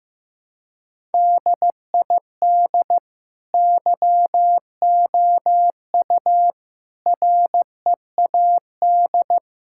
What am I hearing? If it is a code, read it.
Morse code, heard as a keyed tone, about 15 wpm, 700 Hz. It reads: DID YOU READ
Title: Sentences from Top 1000 Words - Review Part 5 - 15wpm